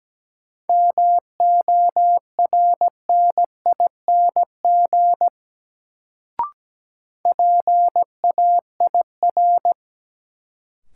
Morse code, heard as a keyed tone, about 17 wpm, 700 Hz. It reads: MORNING PAIR
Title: Sets of 3 Words - New Word Drill - 17wpm